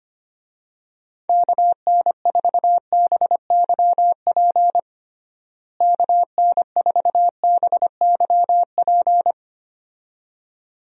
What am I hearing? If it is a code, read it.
Morse code, heard as a keyed tone, about 25 wpm, 700 Hz. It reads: KN4BYP KN4BYP
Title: USA Call Signs - 6 Characters - Repeated 2x - 25wpm